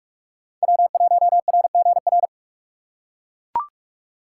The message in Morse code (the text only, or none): W1RGR